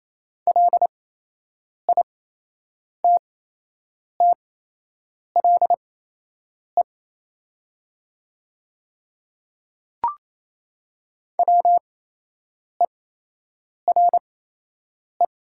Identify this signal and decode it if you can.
Morse code, heard as a keyed tone, about 28 wpm, spaced as for 8 wpm, 690 Hz. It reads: LITTLE WERE
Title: 8x Character Spacing - Top 300 Words - 28wpm